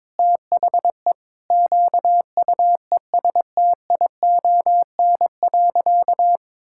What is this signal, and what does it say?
Morse code, heard as a keyed tone, about 21 wpm, 700 Hz. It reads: THE QUESTION.
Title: Sentences from Top 1600 Words 22wpm